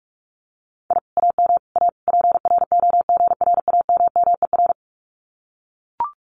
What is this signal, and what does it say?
Morse code, heard as a keyed tone, about 44 wpm, 700 Hz. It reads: I AM A PROGRAMMER
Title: Common QSO Phrases Encore 45wpm